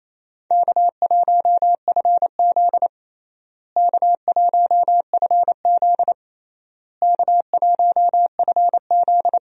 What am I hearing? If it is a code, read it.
Morse code, heard as a keyed tone, about 28 wpm, 700 Hz. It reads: K1FZ K1FZ K1FZ
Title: USA Call Signs - 4 Characters - Repeated 3x - 28wpm